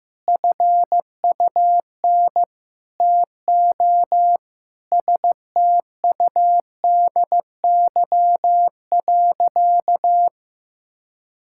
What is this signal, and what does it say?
Morse code, heard as a keyed tone, about 15 wpm, 700 Hz. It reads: FUN TO STUDY.
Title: Sentences from Top 1000 Words - Review Part 5 - 15wpm